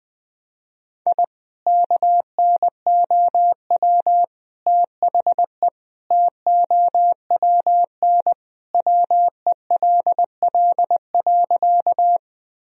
Morse code, heard as a keyed tone, about 20 wpm, 700 Hz. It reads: I KNOW THE TOWN WELL.
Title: Sentences from Top 1000 Words - Review Part 2 - 20wpm